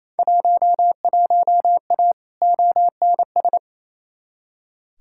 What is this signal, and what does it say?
Morse code, heard as a keyed tone, about 28 wpm, 700 Hz. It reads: 11A ONS
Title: Field Day Exchanges Encore - 28wpm